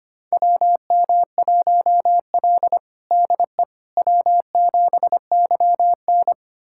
Morse code, heard as a keyed tone, about 25 wpm, 700 Hz. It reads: WM1L DE W7YN
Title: USA Callsign DE Callsign - 4 Characters - 25wpm